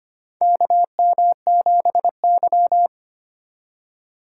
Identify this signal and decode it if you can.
Morse code, heard as a keyed tone, about 25 wpm, 700 Hz. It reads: KM7Y
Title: USA Call Signs - 4 Characters - 2x Character Spacing - 25wpm